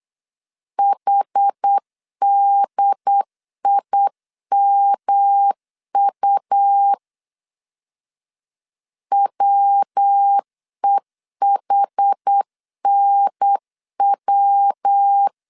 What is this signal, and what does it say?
Morse code, heard as a keyed tone, about 8 wpm, 800 Hz. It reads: HDIMU WEHNW
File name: CW / DET / 1 3 1500C S B 8wpm 4141